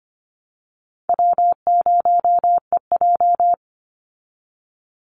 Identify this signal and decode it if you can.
Morse code, heard as a keyed tone, about 25 wpm, 700 Hz. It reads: W0EJ
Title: Easy Call Signs Repeated 2x - 25wpm